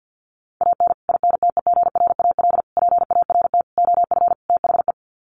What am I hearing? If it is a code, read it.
Morse code, heard as a keyed tone, about 52 wpm, 700 Hz. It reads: AN INTEGRAL PART OF THE
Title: Sets of 5 Words 50wpm